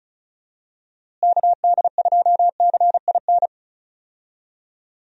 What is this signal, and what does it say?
Morse code, heard as a keyed tone, about 34 wpm, 700 Hz. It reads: KD2CIN
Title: USA Call Signs - 6 Characters - Repeated 3x - 35wpm